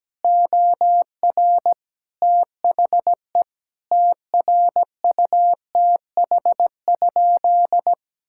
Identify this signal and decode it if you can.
Morse code, heard as a keyed tone, about 17 wpm, 700 Hz. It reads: OR THE TRUTH?
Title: Sentences from Top 1600 Words 17wpm